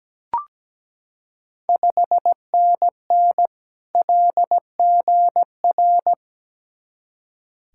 Morse code, heard as a keyed tone, about 17 wpm, 700 Hz. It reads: 5NN LGR
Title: Ontario QSO Party Exchange 17wpm